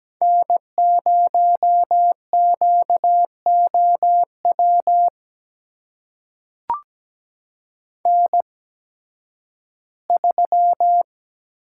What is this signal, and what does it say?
Morse code, heard as a keyed tone, about 17 wpm, spaced as for 11 wpm, 700 Hz. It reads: N0QOW N3
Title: USA Call Signs - 5 Characters - 8x Character Spacing - 17wpm